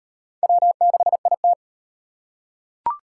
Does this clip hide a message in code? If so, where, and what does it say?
Yes: Morse code from 0.4 to 1.5 s, heard as a keyed tone, about 37 wpm, 700 Hz. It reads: W6IT